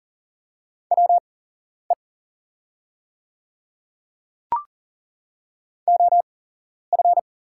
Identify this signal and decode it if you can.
Morse code, heard as a keyed tone, about 40 wpm, spaced as for 11 wpm, 700 Hz. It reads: WE OF
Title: 8x Character Spacing - Top 100 Words - 40wpm